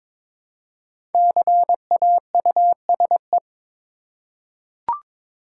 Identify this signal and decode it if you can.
Morse code, heard as a keyed tone, about 22 wpm, 700 Hz. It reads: CAUSE
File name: 5 Letter Words - First 3 Letters Given - 22wpm